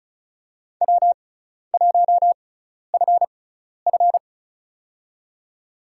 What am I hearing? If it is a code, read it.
Morse code, heard as a keyed tone, about 36 wpm, spaced as for 12 wpm, 700 Hz. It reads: W1FF
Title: USA Call Signs - 6x Character Spacing - 35wpm